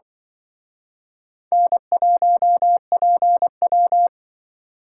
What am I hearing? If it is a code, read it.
Morse code, heard as a keyed tone, about 24 wpm, 700 Hz. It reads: N1PW
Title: Speed Racing - USA 1x2 Call Signs 20wpm